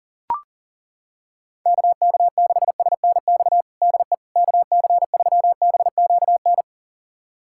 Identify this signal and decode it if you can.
Morse code, heard as a keyed tone, about 37 wpm, 700 Hz. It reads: KK6SNX DE KC3BQN
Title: USA Callsign DE Callsign - 6 Characters - 40wpm